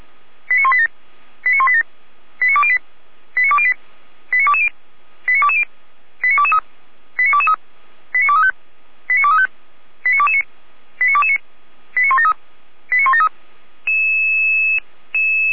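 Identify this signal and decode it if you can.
Five-tone selective call (ZVEI), heard as quick runs of five tones, tones of about 80 ms. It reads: 89187 89187 89208 89208 89220 89220 89223 89223 89235 89235 89229 89229 89173 89173
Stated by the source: ZVEI five-tone selective call received by SDR, server of Feuerwehr Orlamünde fire brigade.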